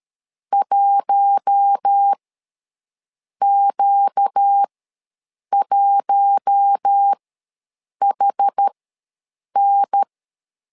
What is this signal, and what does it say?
Morse code, heard as a keyed tone, about 13 wpm, spaced as for 7 wpm, 800 Hz. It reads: 1Q1HN
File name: LEZIONI / 4 Q1 ws 12wpm 2129